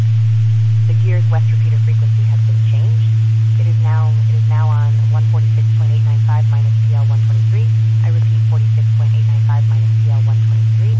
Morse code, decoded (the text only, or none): none